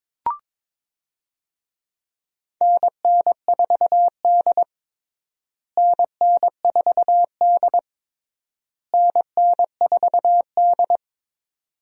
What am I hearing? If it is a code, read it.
Morse code, heard as a keyed tone, about 21 wpm, 700 Hz. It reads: NN4D NN4D NN4D
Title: USA Call Signs - 4 Characters - Repeated 3x - 22wpm